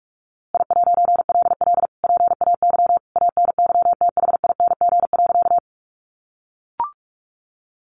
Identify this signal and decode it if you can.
Morse code, heard as a keyed tone, about 46 wpm, 700 Hz. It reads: I'LL PAY ANYTHING.
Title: A Sound Of Thunder 45wpm